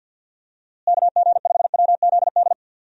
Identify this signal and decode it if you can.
Morse code, heard as a keyed tone, about 50 wpm, 700 Hz. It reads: KG5WZD
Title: USA Call Signs - 6 Characters - Repeated 2x - 50wpm